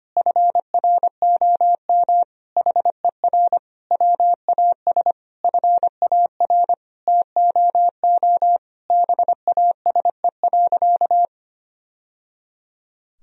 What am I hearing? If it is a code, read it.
Morse code, heard as a keyed tone, about 25 wpm, 700 Hz. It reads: FROM HER WAS FAR TOO BASE.
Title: Sentences from Top 1000 Words - Review Part 1 - 25wpm